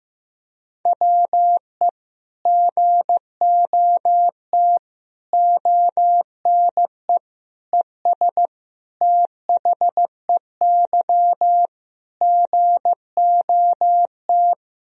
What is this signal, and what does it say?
Morse code, heard as a keyed tone, about 15 wpm, 700 Hz. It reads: WE GOT ONE ES THEY GOT
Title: Sentences from Top 300 Words and Abbreviations 15wpm